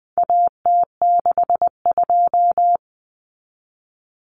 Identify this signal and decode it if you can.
Morse code, heard as a keyed tone, about 20 wpm, 700 Hz. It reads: AT62
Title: Common Cut Numbers - 4-Digit 20wpm